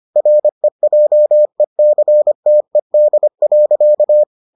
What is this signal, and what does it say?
Morse code, heard as a keyed tone, about 25 wpm, 600 Hz. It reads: REJECTED.